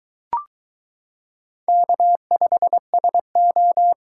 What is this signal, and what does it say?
Morse code, heard as a keyed tone, about 22 wpm, 700 Hz. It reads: K5SO